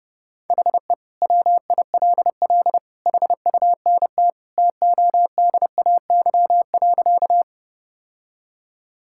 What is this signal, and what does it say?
Morse code, heard as a keyed tone, about 30 wpm, 700 Hz. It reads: HE WILL HUNT TODAY.
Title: Sentences from Top 1600 Words 30wpm